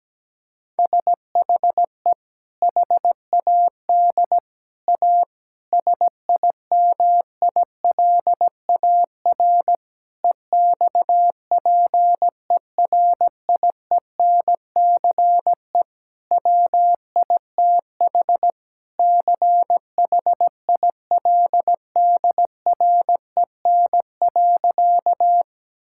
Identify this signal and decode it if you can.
Morse code, heard as a keyed tone, about 17 wpm, 700 Hz. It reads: SHE HAD A SIMILAR EXPERIENCE WITH CHILDREN.
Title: Sentences from Top 1000 Words - Review Part 5 - 17wpm